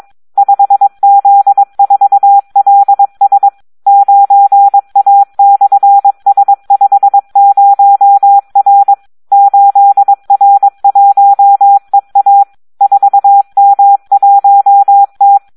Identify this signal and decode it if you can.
Morse code, heard as a keyed tone, about 22 wpm, 800 Hz. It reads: 5Z4LS 9A/S50R 8R1EA 4M1T
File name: Audio practica CW / SD 1 22